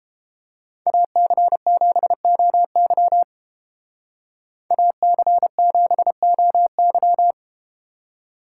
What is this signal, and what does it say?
Morse code, heard as a keyed tone, about 30 wpm, 700 Hz. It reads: AC7OY AC7OY